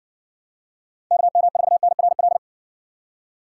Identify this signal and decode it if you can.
Morse code, heard as a keyed tone, about 60 wpm, 700 Hz. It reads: KM4NRL